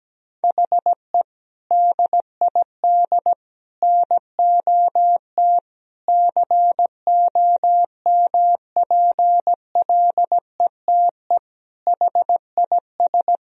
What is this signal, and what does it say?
Morse code, heard as a keyed tone, about 17 wpm, 700 Hz. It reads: HE DID NOT COMPLETE HIS
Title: Sentences from Top 1000 Words - Review Part 1 - 17wpm